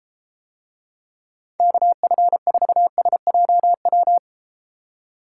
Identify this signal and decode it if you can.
Morse code, heard as a keyed tone, about 33 wpm, 700 Hz. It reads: KF4SJW